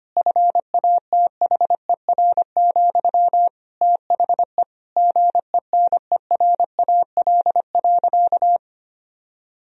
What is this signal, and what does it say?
Morse code, heard as a keyed tone, about 25 wpm, 700 Hz. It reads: FATHER, THE GENERAL.
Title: Sentences from Top 1000 Words - Review Part 3 - 25wpm